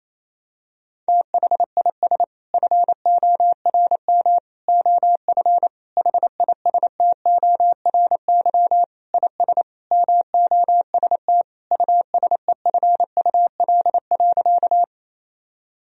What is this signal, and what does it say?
Morse code, heard as a keyed tone, about 28 wpm, 700 Hz. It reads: THIS FORM OF HISTORY IS MOST USEFUL.
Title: Sentences from Top 1000 Words - Review Part 4 - 28wpm